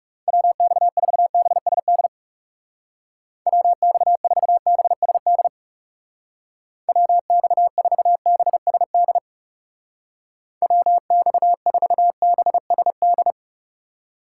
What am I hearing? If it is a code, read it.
Morse code, heard as a keyed tone, about 38 wpm, 700 Hz. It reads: WX4BSD WX4BSD WX4BSD WX4BSD